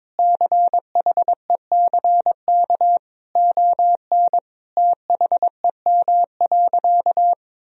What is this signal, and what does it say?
Morse code, heard as a keyed tone, about 21 wpm, 700 Hz. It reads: CHECK ON THEM.
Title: Sentences from Top 1000 Words - Review Part 3 - 22wpm